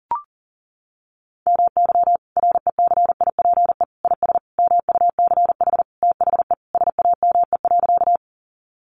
Morse code, heard as a keyed tone, about 40 wpm, 700 Hz. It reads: MY RECIPE IS MUCH THE SAME.
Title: Sentences from Top 1200 Words 40wpm